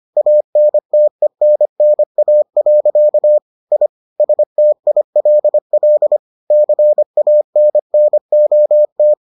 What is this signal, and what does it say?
Morse code, heard as a keyed tone, about 25 wpm, 600 Hz. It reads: ANTENNA. I STILL CANNOT